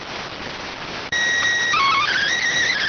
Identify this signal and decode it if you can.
SSTV (slow-scan television), PD 290.